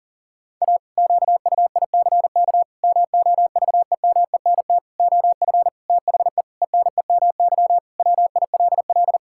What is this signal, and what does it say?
Morse code, heard as a keyed tone, about 40 wpm, 700 Hz. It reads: A QUICK MOVEMENT OF THE ENEMY WILL
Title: Pangram Challenge v2 - 40wpm